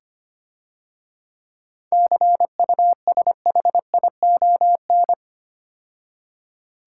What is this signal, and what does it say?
Morse code, heard as a keyed tone, about 28 wpm, 700 Hz. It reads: CUSHION
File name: Easy Words - Speed-Racing with Hints - 25wpm